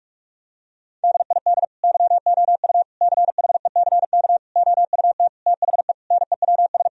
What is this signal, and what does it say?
Morse code, heard as a keyed tone, about 41 wpm, 700 Hz. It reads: DID YOU CHECK OUT THE NEWS
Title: Sentences from Top 1000 Words - Review Part 4 - 45wpm